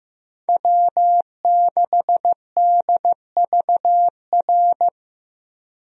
Morse code, heard as a keyed tone, about 15 wpm, 700 Hz. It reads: W6DVR
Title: USA Call Signs - 5 Characters - 15wpm